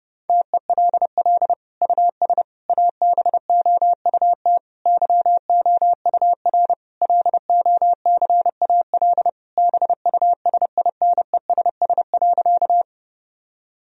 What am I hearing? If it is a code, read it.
Morse code, heard as a keyed tone, about 30 wpm, 700 Hz. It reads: TELL US ABOUT YOUR LOCAL BUSINESS.
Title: Sentences from Top 1000 Words - Review Part 3 - 30wpm